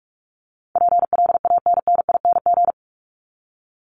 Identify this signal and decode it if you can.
Morse code, heard as a keyed tone, about 44 wpm, 700 Hz. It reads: PLANNING
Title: ING Suffix - Top 100 Words with Base Words 45wpm